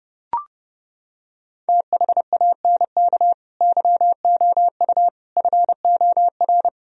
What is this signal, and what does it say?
Morse code, heard as a keyed tone, about 28 wpm, 700 Hz. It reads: THANK YOU FOR